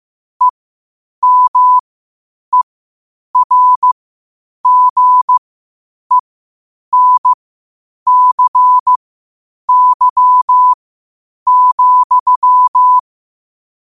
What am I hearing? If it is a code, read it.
Morse code, heard as a keyed tone, about 15 wpm, spaced as for 8 wpm, 1000 Hz. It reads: EMERGENCY,
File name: qotdfiles / 8wpm